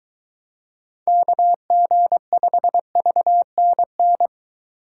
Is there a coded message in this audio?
Yes: Morse code from 1.1 to 4.3 s, heard as a keyed tone, about 23 wpm, 700 Hz. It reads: KG5VNN